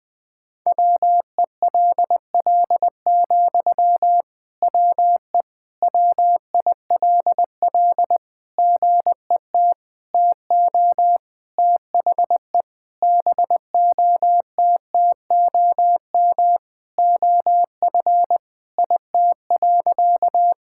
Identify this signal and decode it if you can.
Morse code, heard as a keyed tone, about 21 wpm, 700 Hz. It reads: WELL, WE WILL GET TO THE BOTTOM OF IT.
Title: Sentences from Top 1000 Words - Review Part 6 - 20wpm